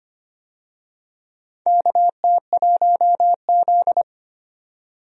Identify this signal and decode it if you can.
Morse code, heard as a keyed tone, about 25 wpm, 700 Hz. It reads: KT1Z
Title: USA Call Signs - 4 Characters - 25wpm